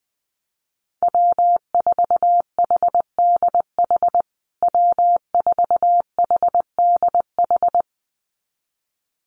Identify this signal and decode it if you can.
Morse code, heard as a keyed tone, about 19 wpm, 700 Hz. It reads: W4HDH W4HDH
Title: USA Call Signs Repeated 2x 20wpm